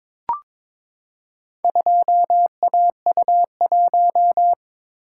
Morse code, heard as a keyed tone, about 22 wpm, 700 Hz. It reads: 2AU1
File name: practice / 22wpm / Four Letter-Number Combinations - 22wpm